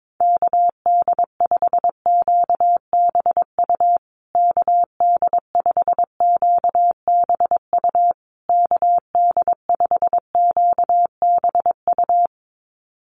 Morse code, heard as a keyed tone, about 22 wpm, 700 Hz. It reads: KD5QBU KD5QBU KD5QBU
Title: USA Call Signs Repeated 3x 22wpm